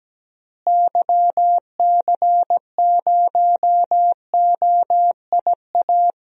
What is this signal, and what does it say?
Morse code, heard as a keyed tone, about 17 wpm, 700 Hz. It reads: YC0OIA
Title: DXCC-100 Call Signs - 6 Characters - Repeated 2x - 17wpm